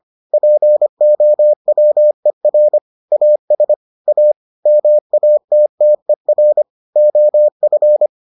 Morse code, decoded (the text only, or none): POWER AS A MATTER OF